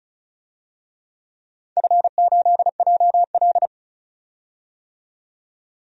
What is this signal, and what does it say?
Morse code, heard as a keyed tone, about 36 wpm, 700 Hz. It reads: F8JL
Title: DXCC-100 Call Signs - 4 Characters - 35wpm